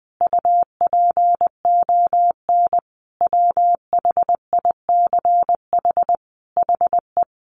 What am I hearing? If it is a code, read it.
Morse code, heard as a keyed tone, about 20 wpm, 700 Hz. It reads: UPON WHICH HE
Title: Sentences from Top 1200 Words 20wpm